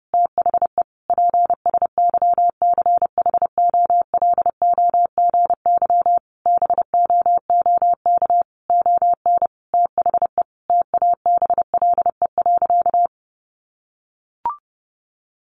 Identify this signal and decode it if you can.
Morse code, heard as a keyed tone, about 28 wpm, 700 Hz. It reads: THE PSYCHOLOGY BOOK ON THE TABLE.
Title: Sentences from Top 1400 Words 30wpm